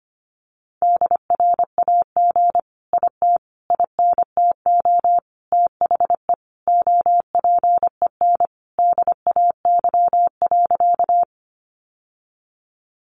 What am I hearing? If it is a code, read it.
Morse code, heard as a keyed tone, about 25 wpm, 700 Hz. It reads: DRAG IT INTO THE OPEN DAY.
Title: Sentences from Top 1500 Words 25wpm